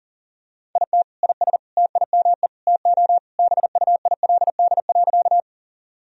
Morse code, heard as a keyed tone, about 40 wpm, 700 Hz. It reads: IT IS TIME TO BUILD.
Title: Sentences from Top 1000 Words - Review Part 1 - 40wpm